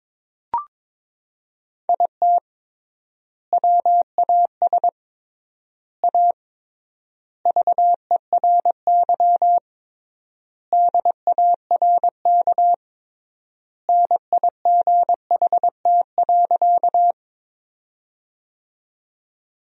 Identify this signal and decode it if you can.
Morse code, heard as a keyed tone, about 22 wpm, 700 Hz. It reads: IT WAS A VERY DARK NIGHT.
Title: 3x Word-Spacing - Sentences from Top 500 Words - 22wpm